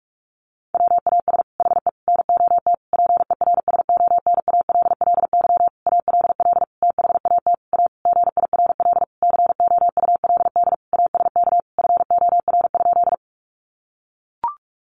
Morse code, heard as a keyed tone, about 44 wpm, 700 Hz. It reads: WAS HE NOT PERSONALLY ALL THAT A GIRL COULD ASK FOR?
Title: Sentences from Top 1400 Words 45wpm